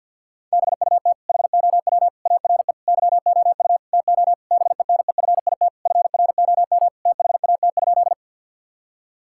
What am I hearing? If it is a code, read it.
Morse code, heard as a keyed tone, about 51 wpm, 700 Hz. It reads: BUT HOW ARE YOU TO BENEFIT FROM THAT?
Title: Sentences from Top 1000 Words - Review Part 6 - 50wpm